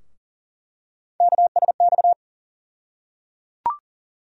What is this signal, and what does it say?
Morse code, heard as a keyed tone, about 37 wpm, 700 Hz. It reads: KSX
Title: USA Call Sign Postfixes - 3 Letters 40wpm